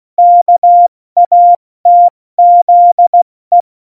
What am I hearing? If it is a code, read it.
Morse code, heard as a keyed tone, about 16 wpm, 700 Hz. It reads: KATZE
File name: Morsekurs / Witze / Witz 18. September 2018, 70 F80